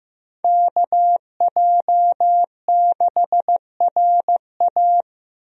Morse code, heard as a keyed tone, about 15 wpm, 700 Hz. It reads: KJ6RA
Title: USA Call Signs - 5 Characters - Repeated 3x - 15wpm